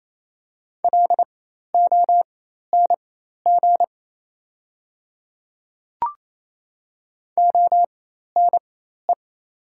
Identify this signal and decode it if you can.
Morse code, heard as a keyed tone, about 28 wpm, spaced as for 13 wpm, 700 Hz. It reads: LONG ONE
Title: 4x Character Spacing - Top 100 Words - 28wpm